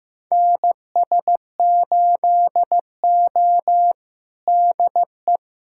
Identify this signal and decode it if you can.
Morse code, heard as a keyed tone, about 15 wpm, 700 Hz. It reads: NS8O DE
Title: USA Callsign DE Callsign - 4 Characters - 15wpm